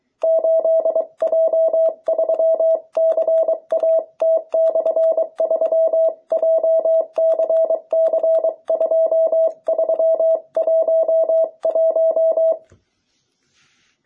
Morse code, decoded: 8J3CAT/3 JCC2311

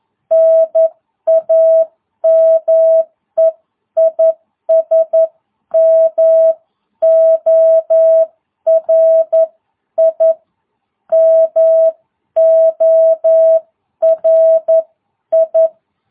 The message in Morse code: NAME IS MORI MORI

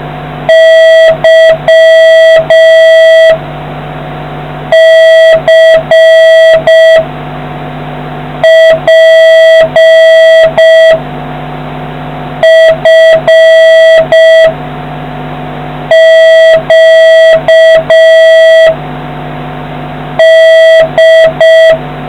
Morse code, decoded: Y C P F Q D

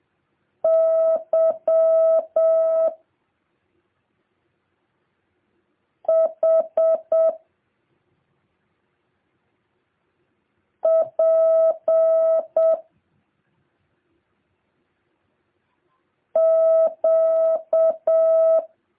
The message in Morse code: Y H P Q

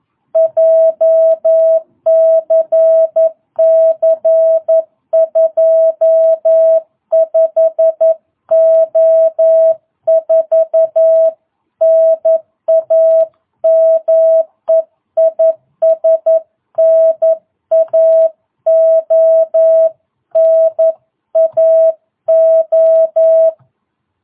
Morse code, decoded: JCC 25O4 NAME IS NAO NAO

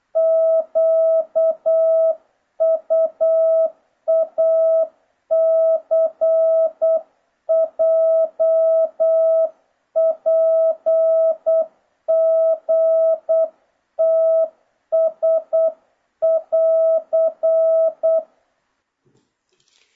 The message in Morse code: QUACJPGTS<AR>

